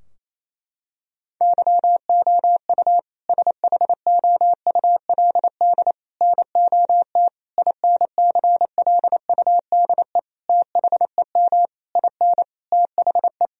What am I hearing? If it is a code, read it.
Morse code, heard as a keyed tone, about 28 wpm, 700 Hz. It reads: YOU SHOULD NOT INCLUDE THEM IN THE
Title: Sentences from Top 1000 Words - Review Part 3 - 28wpm